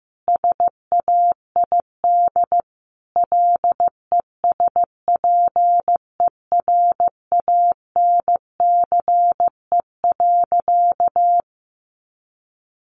Morse code, heard as a keyed tone, about 15 wpm, 700 Hz. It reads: SAID LESPERANCE.
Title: A Sound Of Thunder 15wpm